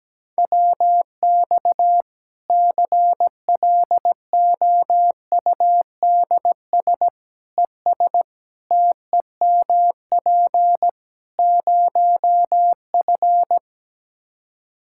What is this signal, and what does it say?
Morse code, heard as a keyed tone, about 17 wpm, 700 Hz. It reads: WX CLOUDS ES TEMP 0F
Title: Practice QSO - Exchange 2 - Info - 17wpm